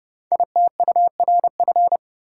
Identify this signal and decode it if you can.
Morse code, heard as a keyed tone, about 30 wpm, 700 Hz. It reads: ITURF
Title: Code Groups - 5 Letters - 30wpm